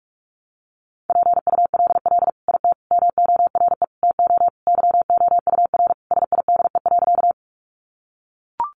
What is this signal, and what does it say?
Morse code, heard as a keyed tone, about 44 wpm, 700 Hz. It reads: PULL IT MORE TO YOUR SIDE.
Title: Sentences from Top 400 Words 45wpm